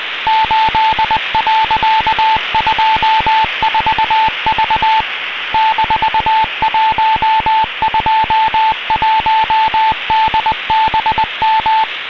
Morse code, heard as a keyed tone, about 20 wpm, 880 Hz, with radio noise. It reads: 8.24V -121DBM
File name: xatcobeo 15h23 utc DK3WN